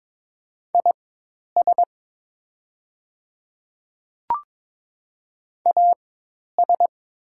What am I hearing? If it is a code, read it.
Morse code, heard as a keyed tone, about 22 wpm, spaced as for 10 wpm, 700 Hz. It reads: IS AS
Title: 4x Character Spacing - Top 200 Words - 22wpm